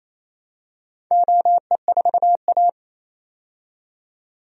Morse code, heard as a keyed tone, about 28 wpm, 700 Hz. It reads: OE4A
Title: DXCC-100 Call Signs - 4 Characters - 28wpm